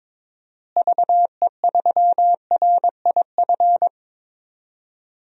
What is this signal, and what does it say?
Morse code, heard as a keyed tone, about 22 wpm, 700 Hz. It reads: VE3RIF
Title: DXCC-100 Call Signs - 6 Characters - Repeated 3x - 22wpm